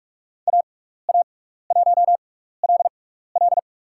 Morse code, heard as a keyed tone, about 44 wpm, spaced as for 15 wpm, 700 Hz. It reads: AA1LL
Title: USA Call Signs - 5 Characters - 6x Character Spacing - 45wpm